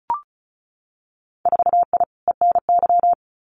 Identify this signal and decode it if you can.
Morse code, heard as a keyed tone, about 36 wpm, 700 Hz. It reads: 4I ENY
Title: Winter Field Day Exchanges Encore 35wpm